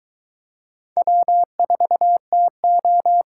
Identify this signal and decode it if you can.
Morse code, heard as a keyed tone, about 22 wpm, 700 Hz. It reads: W4TO